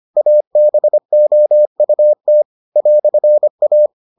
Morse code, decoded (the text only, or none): ABOUT "A